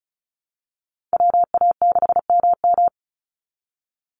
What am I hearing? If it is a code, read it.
Morse code, heard as a keyed tone, about 34 wpm, 700 Hz. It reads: WA6MM
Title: NA SOTA Chaser Callsigns 35wpm